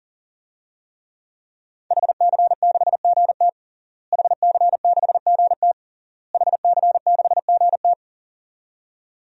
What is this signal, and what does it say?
Morse code, heard as a keyed tone, about 40 wpm, 700 Hz. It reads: HC6GT HC6GT HC6GT
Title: DXCC-100 Call Signs - 5 Characters - Repeated 3x - 40wpm